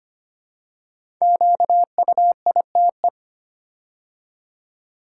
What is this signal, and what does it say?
Morse code, heard as a keyed tone, about 25 wpm, 700 Hz. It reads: QUITE
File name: 5 Letter Words - First 3 Letters Given - 25wpm